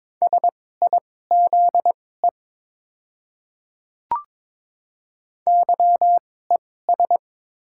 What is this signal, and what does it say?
Morse code, heard as a keyed tone, about 21 wpm, spaced as for 16 wpm, 700 Hz. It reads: SIZE YES
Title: 2x Character Spacing - Top 500 Words - 22wpm